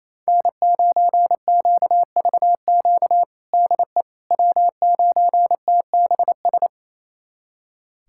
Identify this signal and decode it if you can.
Morse code, heard as a keyed tone, about 28 wpm, 700 Hz. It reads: N9QVQ DE W9TBS
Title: USA Callsign DE Callsign - 5 Characters - 28wpm